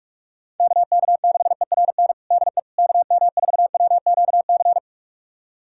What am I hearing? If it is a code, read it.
Morse code, heard as a keyed tone, about 46 wpm, 700 Hz. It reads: KK6ERN DE KM4WQC
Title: USA Callsign DE Callsign - 6 Characters - 45wpm